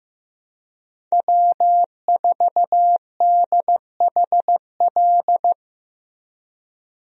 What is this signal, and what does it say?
Morse code, heard as a keyed tone, about 15 wpm, 700 Hz. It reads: W4DHL
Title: USA Call Signs - 5 Characters - Repeated 2x - 15wpm